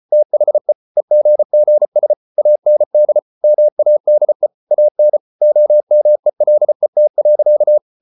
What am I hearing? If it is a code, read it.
Morse code, heard as a keyed tone, about 34 wpm, 600 Hz. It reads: THE EGGS AND MADE AN OMELET.